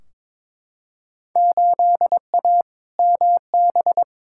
Morse code, heard as a keyed tone, about 22 wpm, 700 Hz. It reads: 8A MB
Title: Field Day Exchanges Encore - 22wpm